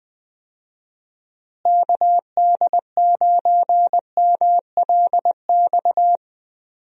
Morse code, heard as a keyed tone, about 21 wpm, 700 Hz. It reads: KD9MLX
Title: USA Call Signs - 6 Characters - Repeated 3x - 20wpm